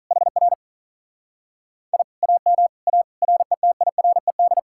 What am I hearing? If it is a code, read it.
Morse code, heard as a keyed tone, about 39 wpm, 700 Hz. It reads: HR I AM A RETIRED
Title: Speed-Racing Common QSO Phrases - 35wpm